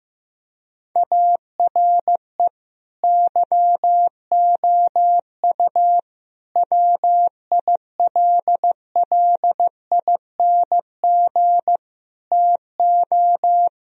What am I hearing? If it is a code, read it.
Morse code, heard as a keyed tone, about 15 wpm, 700 Hz. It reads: ARE YOU WILLING TO
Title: Sentences from Top 1000 Words - Review Part 4 - 15wpm